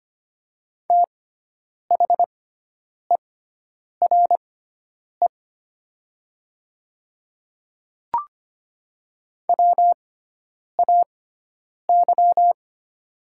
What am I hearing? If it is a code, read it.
Morse code, heard as a keyed tone, about 25 wpm, spaced as for 9 wpm, 700 Hz. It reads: THERE WAY
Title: 6x Character Spacing - Top 100 Words - 25wpm